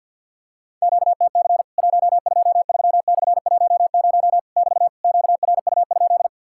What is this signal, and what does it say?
Morse code, heard as a keyed tone, about 50 wpm, 700 Hz. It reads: QTC 123/10 <BT> QRU?
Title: Worked All Europe DX Contest - QTC Exchange - 50wpm